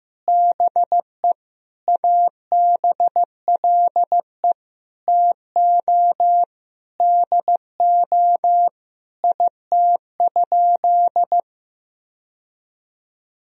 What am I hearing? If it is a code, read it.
Morse code, heard as a keyed tone, about 15 wpm, 700 Hz. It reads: BE ABLE TO DO IT?
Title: Sentences from Top 1000 Words - Review Part 2 - 15wpm